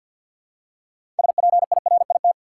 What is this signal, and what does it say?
Morse code, heard as a keyed tone, about 46 wpm, 700 Hz. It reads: SPIRIT
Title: Six Letter Words - 50wpm